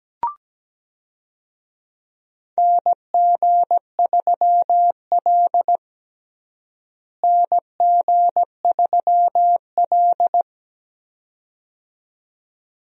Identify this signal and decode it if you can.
Morse code, heard as a keyed tone, about 17 wpm, 700 Hz. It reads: NG3L NG3L
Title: USA Call Signs - 4 Characters - Repeated 2x - 17wpm